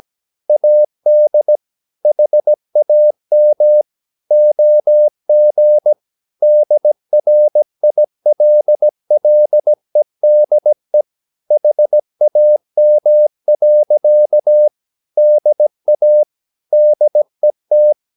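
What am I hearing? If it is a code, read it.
Morse code, heard as a keyed tone, about 17 wpm, 600 Hz. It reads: AD HAM OG DRILLEDE HAM. DA DET
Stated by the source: Slow (QRS) Morse practice at 17 wpm, from CW med Gnister.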